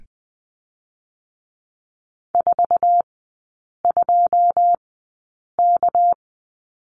Morse code, heard as a keyed tone, about 20 wpm, spaced as for 8 wpm, 700 Hz. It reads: 42K
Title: 198 Lesson 32e - Letter k - Three Characters 20wpm